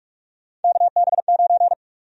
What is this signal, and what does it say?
Morse code, heard as a keyed tone, about 43 wpm, 700 Hz. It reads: KB9
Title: USA Call Sign Prefixes - Repeated 2x - 45wpm